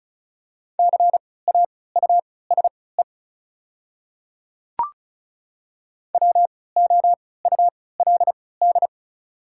Morse code, heard as a keyed tone, about 36 wpm, spaced as for 20 wpm, 690 Hz. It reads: CAUSE WOULD
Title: 3x Character Spacing - Top 500 Words - 35wpm